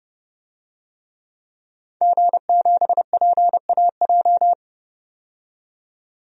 Morse code, heard as a keyed tone, about 29 wpm, 700 Hz. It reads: G7PAJ
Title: DXCC-100 Call Signs - 5 Characters - 30wpm